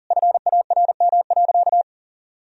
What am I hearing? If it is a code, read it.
Morse code, heard as a keyed tone, about 40 wpm, 700 Hz. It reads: FARM.
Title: Sentences from Top 1000 Words - Review Part 2 - 40wpm